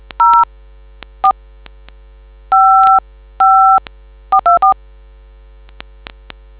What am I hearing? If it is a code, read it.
DTMF, heard as telephone touch tones, tones of about 140 ms. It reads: 04 55424